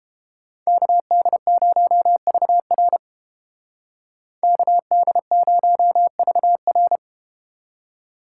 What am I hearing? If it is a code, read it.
Morse code, heard as a keyed tone, about 31 wpm, 700 Hz. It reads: KD0VR KD0VR